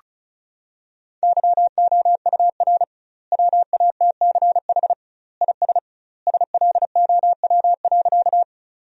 Morse code, heard as a keyed tone, about 34 wpm, 700 Hz. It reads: YOUR WATCH IS SLOW.
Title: 2x Word-Spacing - Sentences from Top 400 Words - 35wpm